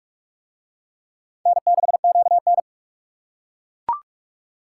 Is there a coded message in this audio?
Yes: Morse code from 1.5 to 2.6 s, heard as a keyed tone, about 41 wpm, 700 Hz. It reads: N6QN